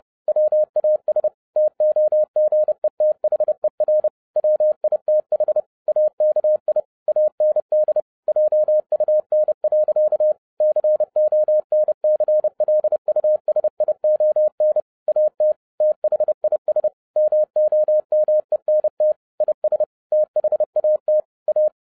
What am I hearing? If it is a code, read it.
Morse code, heard as a keyed tone, about 30 wpm, 610 Hz. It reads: WAS TOGETHER WITH AKI AND JUN. CONCLUSION AT THIS MOMENT IS THAT A
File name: 2011-07-mp3 / n60 BACK FROM A BIG ACCIDENT 30wpm